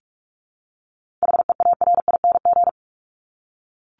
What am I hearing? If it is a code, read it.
Morse code, heard as a keyed tone, about 44 wpm, 700 Hz. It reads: HEARING
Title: ING Suffix - Top 100 Words with Base Words 45wpm